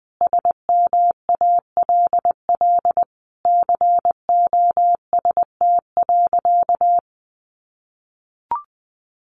Morse code, heard as a keyed tone, about 20 wpm, 700 Hz. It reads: SMALL COST.
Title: Sentences from Top 900 Words 20wpm